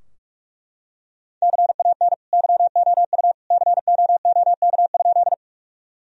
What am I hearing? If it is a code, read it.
Morse code, heard as a keyed tone, about 41 wpm, 700 Hz. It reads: CAN YOU COOK?
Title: Sentences from Top 1000 Words - Review Part 4 - 45wpm